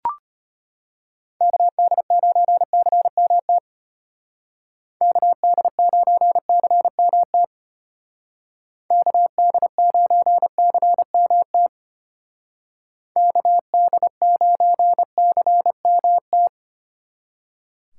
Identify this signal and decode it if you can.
Morse code, heard as a keyed tone, about 30 wpm, 700 Hz. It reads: KD9CMT KD9CMT KD9CMT KD9CMT
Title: Speed Racing - USA 2x3 Call Signs 25wpm